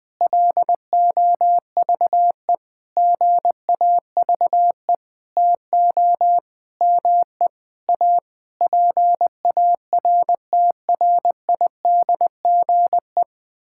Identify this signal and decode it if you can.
Morse code, heard as a keyed tone, about 19 wpm, 700 Hz. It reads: LOVE GAVE TO ME A PARTRIDGE
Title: Sentences with 9-Word Phrases - 20wpm